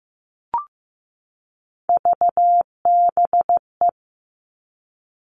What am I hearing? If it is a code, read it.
Morse code, heard as a keyed tone, about 15 wpm, 700 Hz. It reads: VBE
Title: Three Challenging Letter-Number Combinations 15wpm